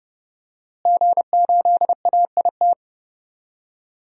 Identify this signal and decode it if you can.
Morse code, heard as a keyed tone, about 30 wpm, 700 Hz. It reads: G8AIT
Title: DXCC-100 Call Signs - 5 Characters - Repeated 3x - 30wpm